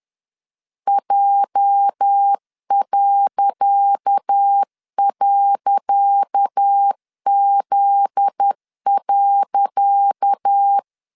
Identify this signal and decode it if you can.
Morse code, heard as a keyed tone, about 10 wpm, 800 Hz. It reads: J..Z.
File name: LEZIONI / 10 SIGNS / 10C DOT 10wpm 2940